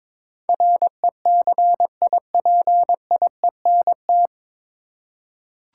Morse code, heard as a keyed tone, about 23 wpm, 700 Hz. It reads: RECIPIENT
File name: 9 Letter Words - First 3 Letters Given - 22wpm